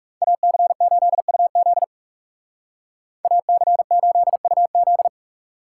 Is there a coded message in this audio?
Yes: Morse code from 0.2 to 5.1 s, heard as a keyed tone, about 39 wpm, 700 Hz. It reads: AC8UZ AC8UZ